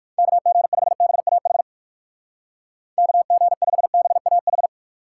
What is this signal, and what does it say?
Morse code, heard as a keyed tone, about 59 wpm, 700 Hz. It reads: KG5BAH KG5BAH